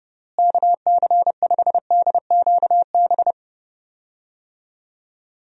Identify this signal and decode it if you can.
Morse code, heard as a keyed tone, about 32 wpm, 700 Hz. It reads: KC5DQB
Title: USA Call Signs - 6 Characters - Repeated 3x - 30wpm